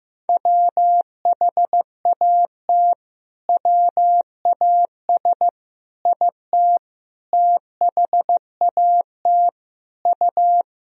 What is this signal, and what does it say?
Morse code, heard as a keyed tone, about 15 wpm, 700 Hz. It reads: WHAT WAS IT THAT U
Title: Sentences from Top 300 Words and Abbreviations 15wpm